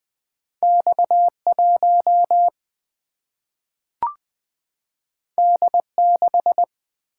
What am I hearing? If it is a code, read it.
Morse code, heard as a keyed tone, about 20 wpm, 700 Hz. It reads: X 1 D 6
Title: Single-Letter Single-Number 20wpm